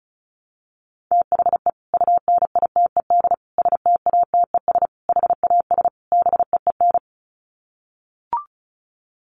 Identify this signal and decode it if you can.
Morse code, heard as a keyed tone, about 34 wpm, 700 Hz. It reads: THE UNITED STATES HAS BEEN
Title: Sets of 5 Words Encore2 35wpm